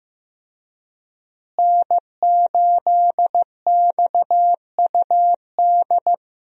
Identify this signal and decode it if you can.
Morse code, heard as a keyed tone, about 15 wpm, 700 Hz. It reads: N8XUD
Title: USA Call Signs - 5 Characters - Repeated 2x - 15wpm